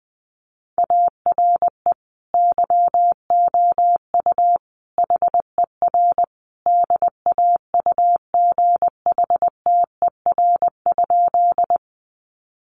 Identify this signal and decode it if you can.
Morse code, heard as a keyed tone, about 20 wpm, 700 Hz. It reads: ARE YOU HER DAUGHTER?
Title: Sentences from Top 1500 Words 20wpm